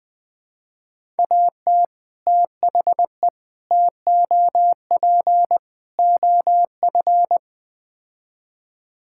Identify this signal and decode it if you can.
Morse code, heard as a keyed tone, about 20 wpm, 700 Hz. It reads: AT THE TOP OF
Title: Build A Phrase - 4-Word Phrases - 20wpm